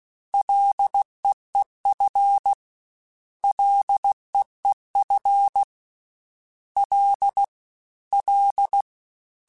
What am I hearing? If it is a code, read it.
Morse code, heard as a keyed tone, about 16 wpm, 800 Hz. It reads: LEEF LEEF L L
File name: F and L / F and L mp3 / F and L Prefixes 3x2 Twice @15